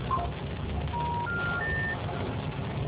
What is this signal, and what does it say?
Special information tone, with noise.